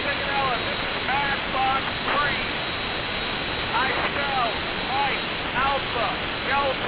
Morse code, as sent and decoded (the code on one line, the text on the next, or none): none